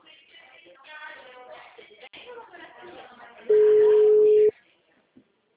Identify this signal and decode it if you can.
Ringback tone (Europe, 425 Hz), with speech.